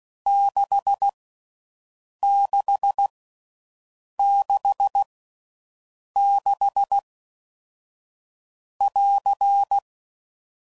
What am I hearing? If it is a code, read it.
Morse code, heard as a keyed tone, about 16 wpm, spaced as for 6 wpm, 800 Hz. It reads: 6666<AR>